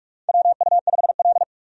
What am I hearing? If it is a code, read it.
Morse code, heard as a keyed tone, about 42 wpm, 700 Hz. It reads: WU5L
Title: USA Call Signs - 4 Characters - 45wpm